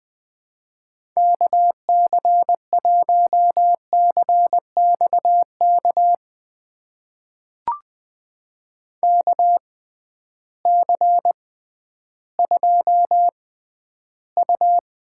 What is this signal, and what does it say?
Morse code, heard as a keyed tone, about 20 wpm, spaced as for 12 wpm, 700 Hz. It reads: KC1CXK KC2U
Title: USA Call Signs - 6 Characters - 6x Character Spacing - 20wpm